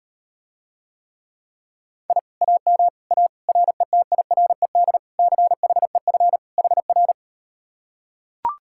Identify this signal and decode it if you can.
Morse code, heard as a keyed tone, about 38 wpm, 700 Hz. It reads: I AM A RETIRED CHEF HR